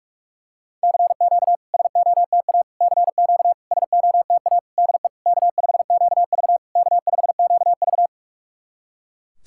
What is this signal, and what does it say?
Morse code, heard as a keyed tone, about 46 wpm, 690 Hz. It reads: CQ SOTA CQ SOTA DE K5QV K5QV
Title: SOTA QSO Practice 45wpm